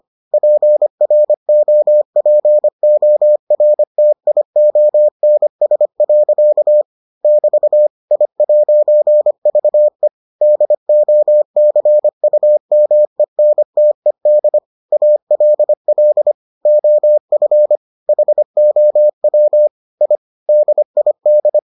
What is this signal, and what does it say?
Morse code, heard as a keyed tone, about 25 wpm, 600 Hz. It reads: PROPORTIONS. <BT> I'VE DOCUMENTED ALL OF HOW I DID